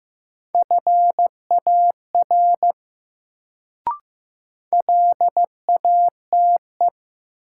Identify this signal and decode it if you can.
Morse code, heard as a keyed tone, about 15 wpm, 700 Hz. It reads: FAR LATE
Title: Sentences from Top 300 Words - New Word Drill - 15wpm